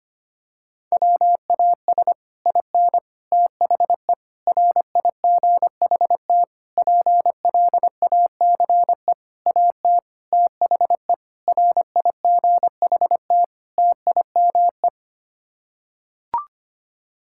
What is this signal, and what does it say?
Morse code, heard as a keyed tone, about 25 wpm, 700 Hz. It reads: WAS IN THE RIGHT PLACE AT THE RIGHT TIME
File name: Sets of 9 Words - 25wpm